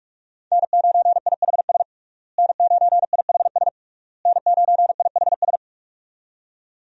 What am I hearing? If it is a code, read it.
Morse code, heard as a keyed tone, about 46 wpm, 700 Hz. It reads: N9IHS N9IHS N9IHS
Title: USA Call Signs - 5 Characters - Repeated 3x - 45wpm